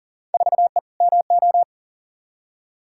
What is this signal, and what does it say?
Morse code, heard as a keyed tone, about 40 wpm, 700 Hz. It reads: 4E MO
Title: Field Day Exchanges Encore - 40wpm